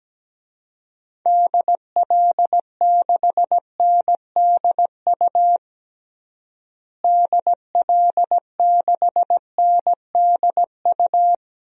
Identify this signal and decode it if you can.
Morse code, heard as a keyed tone, about 17 wpm, 700 Hz. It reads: DL6NDU DL6NDU
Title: DXCC-100 Call Signs - 6 Characters - Repeated 2x - 17wpm